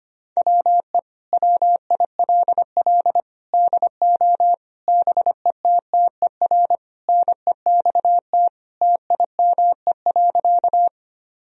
Morse code, heard as a keyed tone, about 25 wpm, 700 Hz. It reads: WE WILL DO BETTER NEXT TIME.
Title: Sentences from Top 1000 Words - Review Part 1 - 25wpm